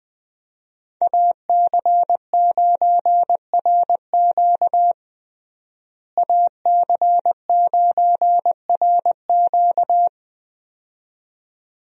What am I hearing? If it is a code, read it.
Morse code, heard as a keyed tone, about 20 wpm, 700 Hz. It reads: AC9RQ AC9RQ
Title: USA Call Signs - 5 Characters - Repeated 2x - 20wpm